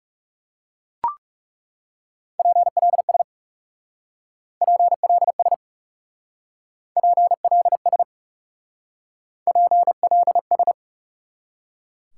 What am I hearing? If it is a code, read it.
Morse code, heard as a keyed tone, about 36 wpm, 700 Hz. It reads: PLS PLS PLS PLS
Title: Speed-Racing Top 90 QSO Elements 30wpm